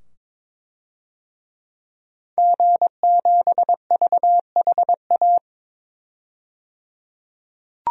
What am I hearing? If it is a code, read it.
Morse code, heard as a keyed tone, about 22 wpm, 700 Hz. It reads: G7VHA
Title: DXCC-100 Call Signs - 5 Characters - 22wpm